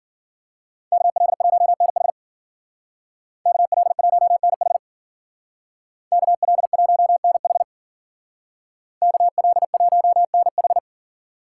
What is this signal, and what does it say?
Morse code, heard as a keyed tone, about 51 wpm, 700 Hz. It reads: KL1NH KL1NH KL1NH KL1NH